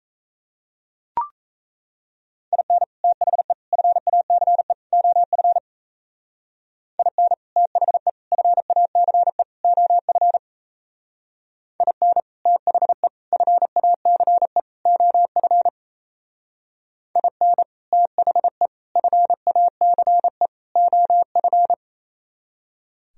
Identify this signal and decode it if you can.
Morse code, heard as a keyed tone, about 37 wpm, 700 Hz. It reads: IN THE FACE OF IN THE FACE OF IN THE FACE OF IN THE FACE OF